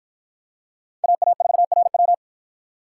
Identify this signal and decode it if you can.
Morse code, heard as a keyed tone, about 50 wpm, 700 Hz. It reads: AA4RW